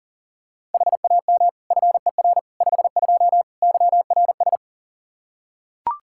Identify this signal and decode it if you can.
Morse code, heard as a keyed tone, about 40 wpm, 700 Hz. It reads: HAM FER 52 YRS
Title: Practice QSO - Exchange 3 - More Info - 40wpm